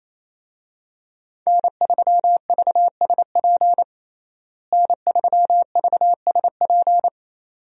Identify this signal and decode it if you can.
Morse code, heard as a keyed tone, about 28 wpm, 700 Hz. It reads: N3VSP N3VSP
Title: USA Call Signs - 5 Characters - Repeated 2x - 28wpm